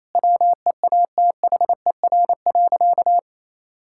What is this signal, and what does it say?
Morse code, heard as a keyed tone, about 28 wpm, 700 Hz. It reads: WEATHER.
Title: Sentences from Top 1000 Words - Review Part 5 - 28wpm